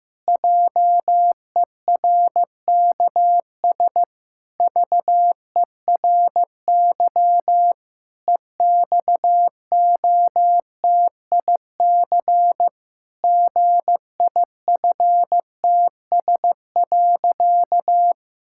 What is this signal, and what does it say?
Morse code, heard as a keyed tone, about 16 wpm, 700 Hz. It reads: JERKS VERY EXOTIC GIFTS.
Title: Pangram Challenge v2 - 15wpm